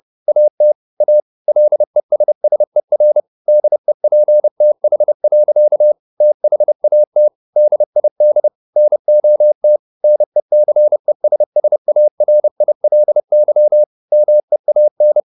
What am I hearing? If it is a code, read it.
Morse code, heard as a keyed tone, about 30 wpm, 600 Hz. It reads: AT A LESSER DEPTH. THAT DID NOT NECESSARILY MEAN
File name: the old man and the sea / the old man and the sea 1 30wpm